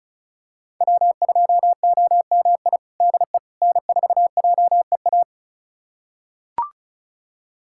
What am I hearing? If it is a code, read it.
Morse code, heard as a keyed tone, about 36 wpm, 700 Hz. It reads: W2OMI DE N4JEA
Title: USA Callsign DE Callsign - 5 Characters - 35wpm